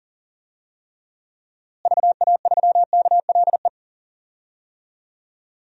Morse code, heard as a keyed tone, about 40 wpm, 700 Hz. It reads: VA3KLE
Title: DXCC-100 Call Signs - 6 Characters - Repeated 3x - 40wpm